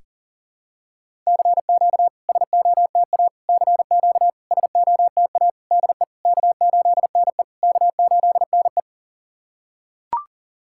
Morse code, heard as a keyed tone, about 40 wpm, 690 Hz. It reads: CQ SOTA CQ SOTA DE K8NE K8NE
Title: SOTA QSO Practice 40wpm